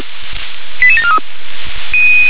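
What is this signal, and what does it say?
Five-tone selective call (ZVEI), heard as quick runs of five tones, tones of about 80 ms. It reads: 88743
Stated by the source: ZVEI five-tone selective call received on an SDR.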